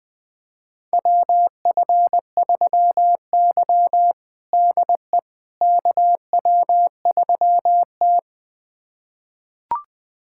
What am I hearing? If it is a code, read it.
Morse code, heard as a keyed tone, about 21 wpm, 700 Hz. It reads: WF3Y DE KW3T
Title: USA Callsign DE Callsign - 4 Characters - 20wpm